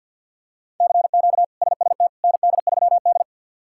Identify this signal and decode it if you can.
Morse code, heard as a keyed tone, about 52 wpm, 700 Hz. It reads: CQ SST ND3D
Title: K1USN SST QSO Practice - 50wpm